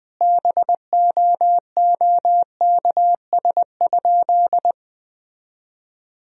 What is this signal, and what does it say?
Morse code, heard as a keyed tone, about 21 wpm, 700 Hz. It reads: BOOKS?
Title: Sentences from Top 1000 Words - Review Part 4 - 20wpm